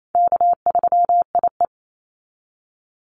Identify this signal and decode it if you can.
Morse code, heard as a keyed tone, about 28 wpm, 700 Hz. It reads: K3IE
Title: Easy Call Signs Repeated 3x - 28wpm